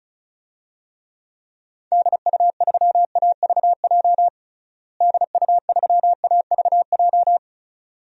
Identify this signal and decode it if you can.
Morse code, heard as a keyed tone, about 36 wpm, 700 Hz. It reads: DU3AVJ DU3AVJ
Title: DXCC-100 Call Signs - 6 Characters - Repeated 2x - 35wpm